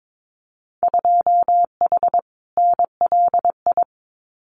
Morse code, heard as a keyed tone, about 22 wpm, 700 Hz. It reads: 2H NLI
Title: Winter Field Day Exchanges 22wpm